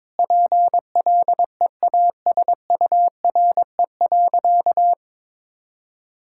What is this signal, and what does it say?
Morse code, heard as a keyed tone, about 22 wpm, 700 Hz. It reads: PLEASURE.
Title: Sentences from Top 1600 Words 22wpm